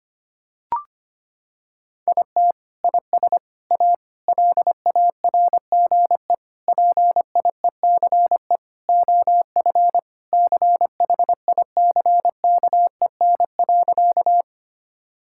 Morse code, heard as a keyed tone, about 25 wpm, 700 Hz. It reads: IT IS A LARGE PIECE OF CHICKEN.
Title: Sentences from Top 1000 Words - Review Part 5 - 25wpm